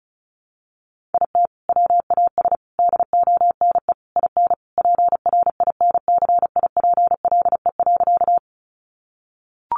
Morse code, heard as a keyed tone, about 34 wpm, 700 Hz. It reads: IT WAS DONE IN PRINCIPLE.
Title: Sentences from Top 1400 Words 35wpm